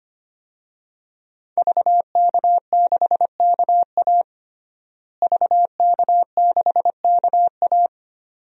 Morse code, heard as a keyed tone, about 25 wpm, 700 Hz. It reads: VK6KA VK6KA
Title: DXCC-100 Call Signs - 5 Characters - Repeated 2x - 25wpm